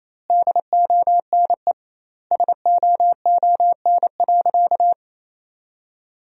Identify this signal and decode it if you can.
Morse code, heard as a keyed tone, about 28 wpm, 700 Hz. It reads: DONE SOON.
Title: 2x Word-Spacing - Sentences from Top 500 Words - 28wpm